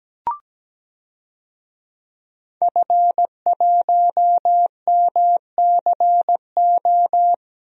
Morse code, heard as a keyed tone, about 17 wpm, 700 Hz. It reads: F1MCO
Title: DXCC-100 Call Signs - 5 Characters - Repeated 2x - 17wpm